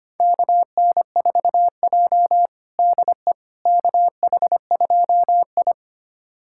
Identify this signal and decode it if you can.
Morse code, heard as a keyed tone, about 25 wpm, 700 Hz. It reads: KN4J DE KH2I
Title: USA Callsign DE Callsign - 4 Characters - 25wpm